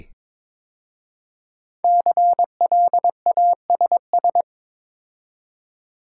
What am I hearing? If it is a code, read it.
Morse code, heard as a keyed tone, about 22 wpm, 700 Hz. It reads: CLASS
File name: 5 Letter Words - First 3 Letters Given - 22wpm